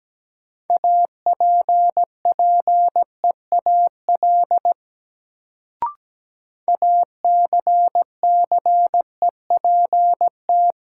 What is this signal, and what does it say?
Morse code, heard as a keyed tone, about 17 wpm, 700 Hz. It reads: APPEAL ACCEPT
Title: Sentences from Top 1300 Words - New Word Drill - 17wpm